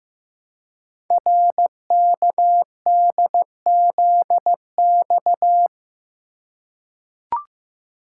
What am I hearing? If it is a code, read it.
Morse code, heard as a keyed tone, about 15 wpm, 700 Hz. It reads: RKDZX
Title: Code Groups - 5 Letters - 15wpm